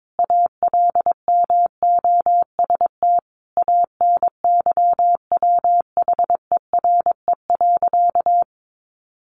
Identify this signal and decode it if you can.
Morse code, heard as a keyed tone, about 22 wpm, 700 Hz. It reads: ALMOST ANYWHERE.
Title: Sentences from Top 1400 Words 22wpm